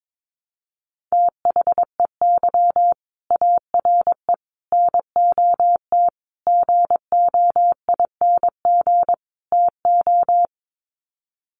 Morse code, heard as a keyed tone, about 22 wpm, 700 Hz. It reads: THEY ARE NOT GOING TO
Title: Sets of 5 Words Encore2 22wpm